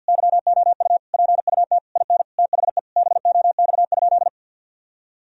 Morse code, heard as a keyed tone, about 52 wpm, 700 Hz. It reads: YOU PUT IN THE BOX?
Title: Sentences from Top 1000 Words - Review Part 2 - 50wpm